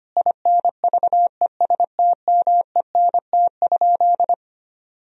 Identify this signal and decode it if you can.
Morse code, heard as a keyed tone, about 25 wpm, 700 Hz. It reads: INVESTMENT?
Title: Sentences from Top 1000 Words - Review Part 4 - 25wpm